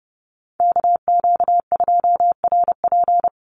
Morse code, heard as a keyed tone, about 30 wpm, 700 Hz. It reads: KQ2RP
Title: NA SOTA Chaser Callsigns 30wpm